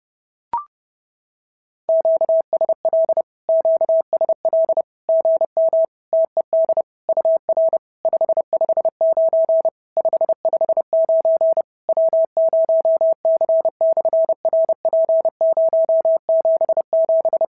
Morse code, heard as a keyed tone, about 30 wpm, 650 Hz. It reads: QSL QSL GM TED UR 559 559 W0C/RP077
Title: SOTA QSO Practice 30wpm